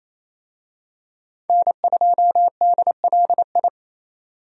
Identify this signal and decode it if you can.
Morse code, heard as a keyed tone, about 28 wpm, 700 Hz. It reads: N2DLI
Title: USA Call Signs - 5 Characters - Repeated 2x - 28wpm